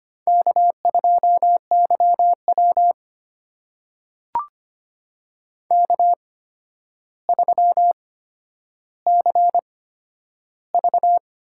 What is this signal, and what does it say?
Morse code, heard as a keyed tone, about 25 wpm, spaced as for 11 wpm, 700 Hz. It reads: K2YW K3CV
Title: USA Call Signs - 4 Characters - 8x Character Spacing - 25wpm